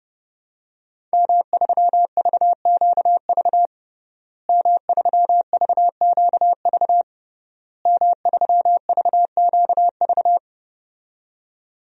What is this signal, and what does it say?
Morse code, heard as a keyed tone, about 30 wpm, 700 Hz. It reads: M3VQV M3VQV M3VQV
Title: DXCC-100 Call Signs - 5 Characters - Repeated 3x - 30wpm